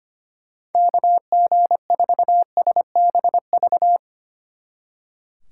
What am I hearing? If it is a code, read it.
Morse code, heard as a keyed tone, about 31 wpm, 700 Hz. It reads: KG4SBV
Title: Speed Racing - USA 2x3 Call Signs 25wpm